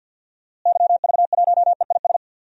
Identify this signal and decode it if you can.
Morse code, heard as a keyed tone, about 49 wpm, 700 Hz. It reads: YV1EIS
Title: DXCC-100 Call Signs - 6 Characters - Repeated 3x - 50wpm